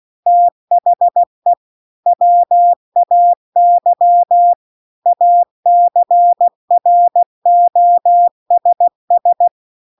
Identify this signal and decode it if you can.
Morse code, heard as a keyed tone, about 16 wpm, 700 Hz. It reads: THE WAY ACROSS